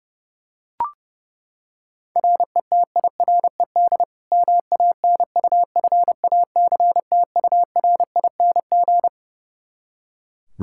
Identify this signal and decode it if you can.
Morse code, heard as a keyed tone, about 32 wpm, 700 Hz. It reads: RETIRED MANUFACTURING
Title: K1USN SST QSO Practice 30wpm